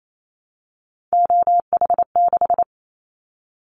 Morse code, heard as a keyed tone, about 28 wpm, 700 Hz. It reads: OH6
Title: Callsign Prefixes for Top 50 DXCC - 28wpm